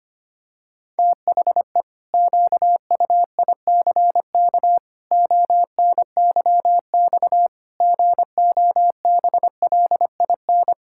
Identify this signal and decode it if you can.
Morse code, heard as a keyed tone, about 25 wpm, 700 Hz. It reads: THE QUICK ONYX GOBLIN
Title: Pangram Challenge v2 - 25wpm